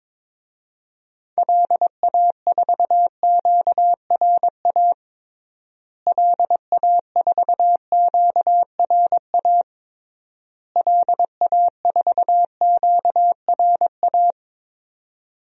Morse code, heard as a keyed tone, about 21 wpm, 700 Hz. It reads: LA4QRA LA4QRA LA4QRA
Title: DXCC-100 Call Signs - 6 Characters - Repeated 3x - 22wpm